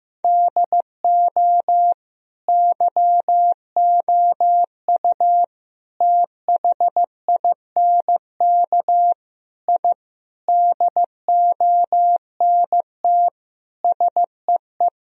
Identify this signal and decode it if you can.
Morse code, heard as a keyed tone, about 15 wpm, 700 Hz. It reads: DO YOU THINK I DONT SEE
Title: Sentences from Top 1000 Words - Review Part 1 - 15wpm